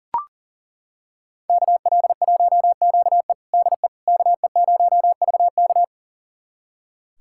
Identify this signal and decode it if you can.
Morse code, heard as a keyed tone, about 40 wpm, 700 Hz. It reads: KL1QE DE KE0VK
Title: USA Callsign DE Callsign - 5 Characters - 40wpm